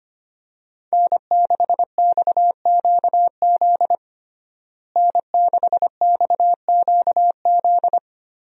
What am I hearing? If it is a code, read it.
Morse code, heard as a keyed tone, about 25 wpm, 700 Hz. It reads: N6XQZ N6XQZ
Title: USA Call Signs - 5 Characters - Repeated 2x - 25wpm